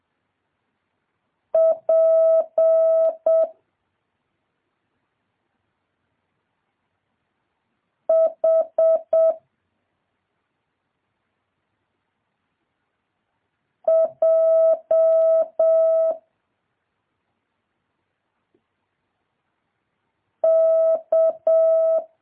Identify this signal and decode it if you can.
Morse code, heard as a keyed tone, about 7 wpm, 650 Hz. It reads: P H J K
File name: jh8cbh / 2023cw